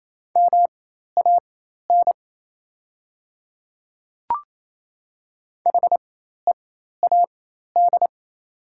Morse code, heard as a keyed tone, about 28 wpm, spaced as for 13 wpm, 700 Hz. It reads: MAN HEAD
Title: 4x Character Spacing - Top 300 Words - 28wpm